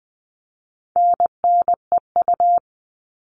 Morse code, heard as a keyed tone, about 20 wpm, 700 Hz. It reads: NNEU
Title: All Cut Numbers - 4-Digit 20wpm